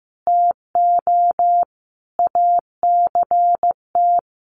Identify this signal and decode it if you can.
Morse code, heard as a keyed tone, about 15 wpm, 700 Hz. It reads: TO ACT
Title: Sentences from Top 500 Words 15wpm